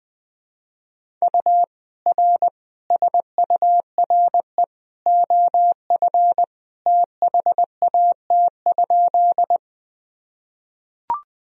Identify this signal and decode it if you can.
Morse code, heard as a keyed tone, about 20 wpm, 700 Hz. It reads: U R SURE OF THAT?
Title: Sentences from Top 300 Words and Abbreviations 20wpm